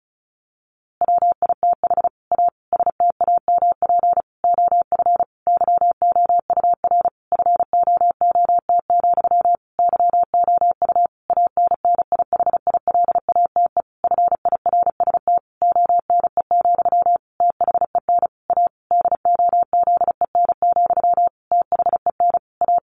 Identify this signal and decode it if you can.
Morse code, heard as a keyed tone, about 34 wpm, 700 Hz. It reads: WITH A STAMP OF YOUR FOOT, YOU ANNIHILATE FIRST ONE, THEN A DOZEN, THEN A
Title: A Sound Of Thunder 35wpm